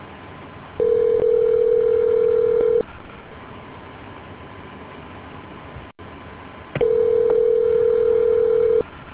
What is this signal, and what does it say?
Ringback tone (North America), 440 + 480 Hz.